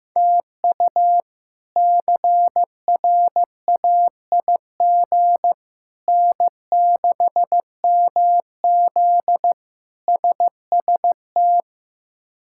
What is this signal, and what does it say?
Morse code, heard as a keyed tone, about 16 wpm, 700 Hz. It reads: TU CRAIG N6MZ SST
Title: K1USN SST QSO Practice - 15wpm